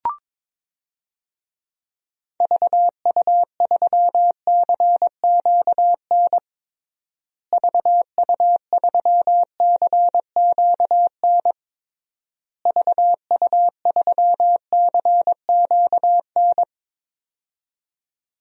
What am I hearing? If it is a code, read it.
Morse code, heard as a keyed tone, about 22 wpm, 700 Hz. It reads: VU3CQN VU3CQN VU3CQN
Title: DXCC-100 Call Signs - 6 Characters - Repeated 3x - 22wpm